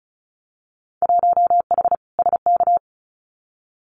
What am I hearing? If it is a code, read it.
Morse code, heard as a keyed tone, about 34 wpm, 700 Hz. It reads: 1H SK
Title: Winter Field Day Exchanges 35wpm